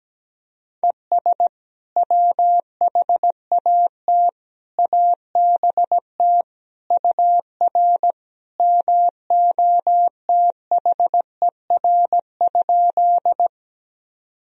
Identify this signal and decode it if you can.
Morse code, heard as a keyed tone, about 17 wpm, 700 Hz. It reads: ES WHAT ABT UR MOTHER?
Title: Sentences from Top 200 Words and Abbreviations 17wpm